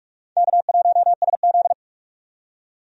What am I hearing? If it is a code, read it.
Morse code, heard as a keyed tone, about 46 wpm, 700 Hz. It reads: K1SZ
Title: USA Call Signs - 4 Characters - 45wpm